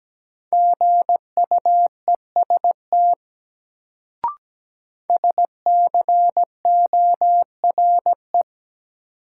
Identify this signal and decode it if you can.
Morse code, heard as a keyed tone, about 17 wpm, 700 Hz. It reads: GUEST SCORE
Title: Sentences from Top 1600 Words - New Word Drill - 17wpm